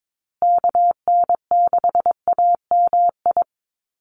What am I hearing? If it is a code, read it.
Morse code, heard as a keyed tone, about 22 wpm, 700 Hz. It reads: KN6AMI
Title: USA Call Signs Repeated 3x 22wpm